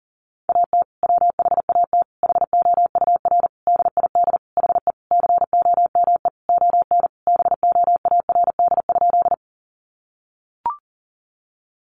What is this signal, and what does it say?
Morse code, heard as a keyed tone, about 40 wpm, 700 Hz. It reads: AT WHAT HOUR DID HE COME ON BOARD?
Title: Sentences from Top 700 Words 40wpm